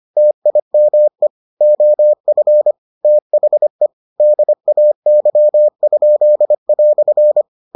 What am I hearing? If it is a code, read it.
Morse code, heard as a keyed tone, about 25 wpm, 600 Hz. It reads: TIME OF THE DAY?"